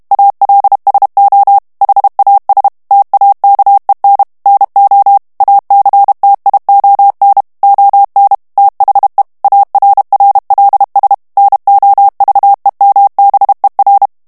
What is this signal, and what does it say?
Morse code, heard as a keyed tone, about 32 wpm, 800 Hz. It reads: ALSO HAS TAKEN NO ACTION ON THE ARRLS NOVEMBER
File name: ANGLAIS texte clair 3 vitesse 30 mots minute